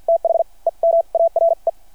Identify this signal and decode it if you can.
Morse code, heard as a keyed tone, about 51 wpm, spaced as for 37 wpm, 660 Hz. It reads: THEM ARE
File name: archived / CRSnet